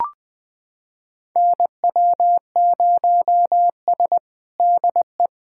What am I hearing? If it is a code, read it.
Morse code, heard as a keyed tone, about 20 wpm, 700 Hz. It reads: NW0S DE
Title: USA Callsign DE Callsign - 4 Characters - 20wpm